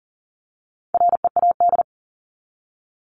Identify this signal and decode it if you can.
Morse code, heard as a keyed tone, about 40 wpm, 700 Hz. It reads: READ
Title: Top 300 Words 40wpm